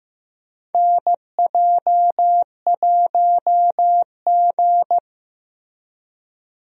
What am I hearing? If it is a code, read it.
Morse code, heard as a keyed tone, about 15 wpm, 700 Hz. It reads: NJ1G
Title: USA Call Signs - 4 Characters - Repeated 2x - 15wpm